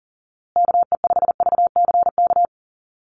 Morse code, heard as a keyed tone, about 40 wpm, 700 Hz. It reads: KE5VCK
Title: USA Call Signs Repeated 2x 40wpm